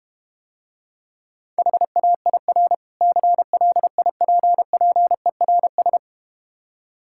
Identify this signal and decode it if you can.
Morse code, heard as a keyed tone, about 31 wpm, 700 Hz. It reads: HAIR CLIPPERS
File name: Difficult Words - Speed-Racing with Hints - 25wpm